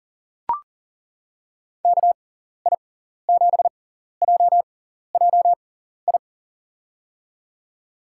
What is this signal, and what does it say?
Morse code, heard as a keyed tone, about 39 wpm, spaced as for 14 wpm, 700 Hz. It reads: KI7JJI
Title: USA Call Signs - 6x Character Spacing - 40wpm